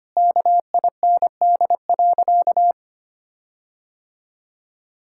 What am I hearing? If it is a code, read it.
Morse code, heard as a keyed tone, about 25 wpm, 700 Hz. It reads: KIND.
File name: practice / 25wpm / Sentences with 4-Word Phrases - 25wpm